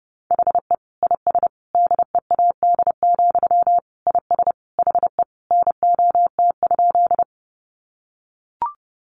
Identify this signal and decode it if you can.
Morse code, heard as a keyed tone, about 28 wpm, 700 Hz. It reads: HE IS DEAD, IS HE NOT?
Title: Sentences from Top 1500 Words 30wpm